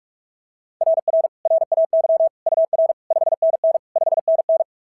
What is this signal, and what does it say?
Morse code, heard as a keyed tone, about 44 wpm, 690 Hz. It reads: RR RAY UR 5NN 5NN
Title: SOTA QSO Practice 45wpm